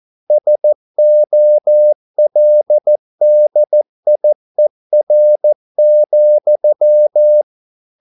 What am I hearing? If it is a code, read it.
Morse code, heard as a keyed tone, about 14 wpm, 600 Hz. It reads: SOLDIER,